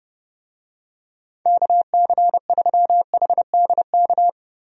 Morse code, heard as a keyed tone, about 30 wpm, 700 Hz. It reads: KC3HDK
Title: USA Call Signs - 6 Characters - 30wpm